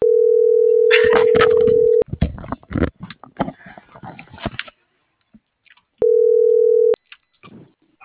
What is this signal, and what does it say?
Ringback tone (North America), 440 + 480 Hz.